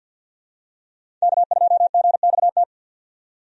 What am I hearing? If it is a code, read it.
Morse code, heard as a keyed tone, about 50 wpm, 700 Hz. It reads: K2GXT
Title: USA Call Signs - 5 Characters - 50wpm